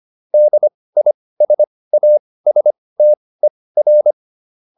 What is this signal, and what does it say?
Morse code, heard as a keyed tone, about 25 wpm, spaced as for 18 wpm, 600 Hz. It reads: DISASTER
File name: DIS8 18